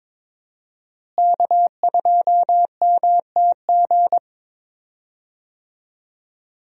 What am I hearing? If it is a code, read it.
Morse code, heard as a keyed tone, about 22 wpm, 700 Hz. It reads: K2MTG
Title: USA Call Signs - 5 Characters - Repeated 3x - 22wpm